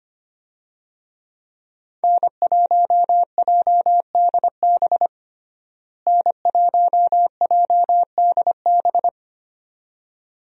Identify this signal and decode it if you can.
Morse code, heard as a keyed tone, about 25 wpm, 700 Hz. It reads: N1JDB N1JDB
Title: USA Call Signs - 5 Characters - Repeated 2x - 25wpm